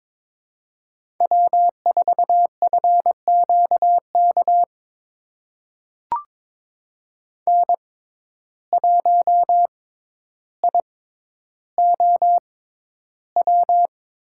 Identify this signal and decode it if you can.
Morse code, heard as a keyed tone, about 22 wpm, spaced as for 11 wpm, 700 Hz. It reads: W4FQK N1IOW
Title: USA Call Signs - 6x Character Spacing - 22wpm